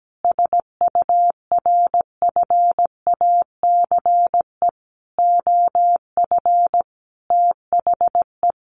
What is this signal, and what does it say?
Morse code, heard as a keyed tone, about 17 wpm, 700 Hz. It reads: SURFACE OF THE
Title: Sentences from Top 1200 Words 17wpm